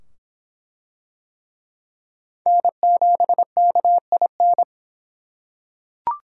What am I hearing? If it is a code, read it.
Morse code, heard as a keyed tone, about 24 wpm, 700 Hz. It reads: N7KIN